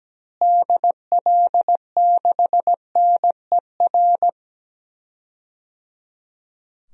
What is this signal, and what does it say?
Morse code, heard as a keyed tone, about 17 wpm, 700 Hz. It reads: DL6NER
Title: DXCC-100 Call Signs - 6 Characters - Repeated 3x - 17wpm